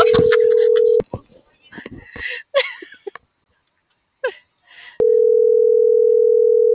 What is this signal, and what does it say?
Ringback tone (North America), 440 + 480 Hz.
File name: CENTRAL GENERAL / out rubyalejandra